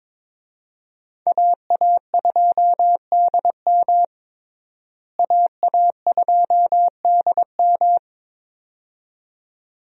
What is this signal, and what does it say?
Morse code, heard as a keyed tone, about 22 wpm, 700 Hz. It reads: AA2DM AA2DM
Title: USA Call Signs - 5 Characters - Repeated 2x - 22wpm